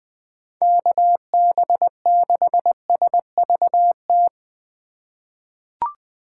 Morse code, heard as a keyed tone, about 20 wpm, 700 Hz. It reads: KB6SVT
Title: USA Call Signs - 6 Characters - 2x Character Spacing - 20wpm